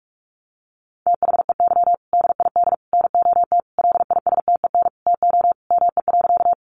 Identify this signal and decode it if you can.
Morse code, heard as a keyed tone, about 44 wpm, 700 Hz. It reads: THEY DID NOT LISTEN TO ME.
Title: Sentences from Top 600 Words 45wpm